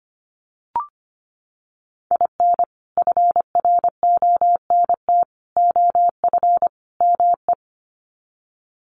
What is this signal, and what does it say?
Morse code, heard as a keyed tone, about 25 wpm, 700 Hz. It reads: IN FRONT OF ME
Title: Sets of 4 Words 25wpm